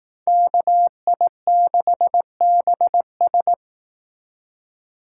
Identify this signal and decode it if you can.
Morse code, heard as a keyed tone, about 18 wpm, 700 Hz. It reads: KI6BS
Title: Speed Racing - USA 2x2 Call Signs 15wpm